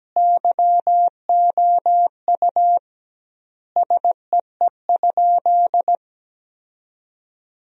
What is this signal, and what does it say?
Morse code, heard as a keyed tone, about 17 wpm, 700 Hz. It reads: YOU SEE?
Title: 2x Word-Spacing - Sentences from Top 100 Words - 17wpm